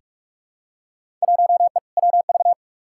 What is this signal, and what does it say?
Morse code, heard as a keyed tone, about 44 wpm, 700 Hz. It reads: 1E WV
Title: Field Day Exchanges Encore - 45wpm